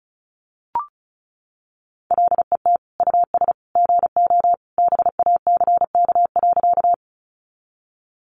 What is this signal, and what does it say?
Morse code, heard as a keyed tone, about 34 wpm, 700 Hz. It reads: LET US GO BACK.
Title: Sentences from Top 300 Words 35wpm